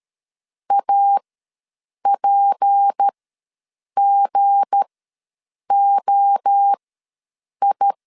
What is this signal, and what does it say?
Morse code, heard as a keyed tone, about 13 wpm, spaced as for 7 wpm, 800 Hz. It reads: APGOI
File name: LEZIONI / 7 GF9Z / 7 G ws 12wpm 3140